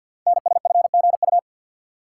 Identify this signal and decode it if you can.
Morse code, heard as a keyed tone, about 51 wpm, 700 Hz. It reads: NSFGU
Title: Code Groups - 5 Letters - 50wpm